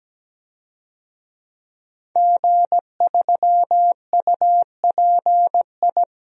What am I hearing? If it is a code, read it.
Morse code, heard as a keyed tone, about 17 wpm, 700 Hz. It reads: G3UPI
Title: DXCC-100 Call Signs - 5 Characters - 17wpm